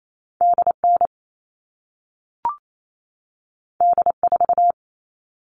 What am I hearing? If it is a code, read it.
Morse code, heard as a keyed tone, about 28 wpm, 700 Hz. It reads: DN D4
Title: All Cut Numbers - 2-Digit 28wpm